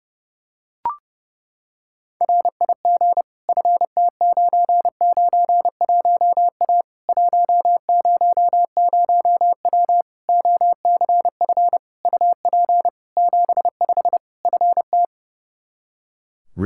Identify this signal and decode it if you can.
Morse code, heard as a keyed tone, about 30 wpm, 700 Hz. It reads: RIG FT991A 100W OCF UP 75 FT
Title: K1USN SST QSO Practice 30wpm